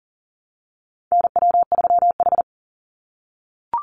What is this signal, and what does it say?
Morse code, heard as a keyed tone, about 40 wpm, 700 Hz. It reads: NW3H
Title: USA Call Signs 40wpm